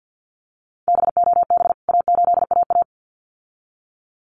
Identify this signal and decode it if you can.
Morse code, heard as a keyed tone, about 48 wpm, 700 Hz. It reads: BOB A8AA
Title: CWT Exchange 50wpm